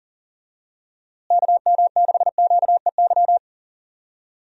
Morse code, heard as a keyed tone, about 37 wpm, 700 Hz. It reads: KM6QEY
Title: USA Call Signs - 6 Characters - Repeated 2x - 40wpm